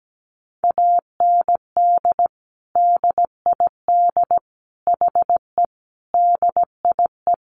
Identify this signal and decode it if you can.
Morse code, heard as a keyed tone, about 17 wpm, 700 Hz. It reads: AND DID HE DIE
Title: Sentences from Top 1300 Words 17wpm